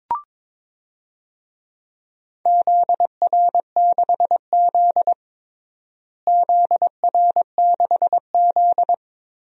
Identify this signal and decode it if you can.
Morse code, heard as a keyed tone, about 22 wpm, 700 Hz. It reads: ZR6Z ZR6Z
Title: DXCC-100 Call Signs - 4 Characters - Repeated 3x - 22wpm